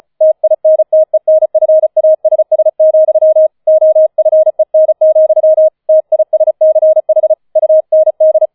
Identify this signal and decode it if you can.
Morse code, heard as a keyed tone, about 34 wpm, 630 Hz. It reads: TINTENFASS, OFEN, TISCH UND
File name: 004 Vierter Streich 35wpm